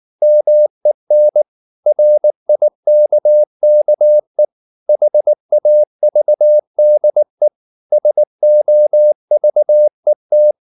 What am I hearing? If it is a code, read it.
Morse code, heard as a keyed tone, about 19 wpm, 600 Hz. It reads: MEN RIKKE HAVDE SOVET
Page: Ferie 19 wpm